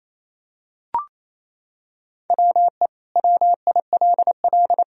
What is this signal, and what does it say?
Morse code, heard as a keyed tone, about 28 wpm, 700 Hz. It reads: WE WILL
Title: Sentences from Top 1000 Words - Review Part 4 - 28wpm